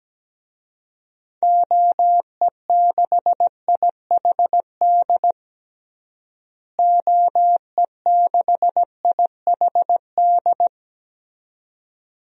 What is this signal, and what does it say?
Morse code, heard as a keyed tone, about 17 wpm, 700 Hz. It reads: OE6IHD OE6IHD
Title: DXCC-100 Call Signs - 6 Characters - Repeated 2x - 17wpm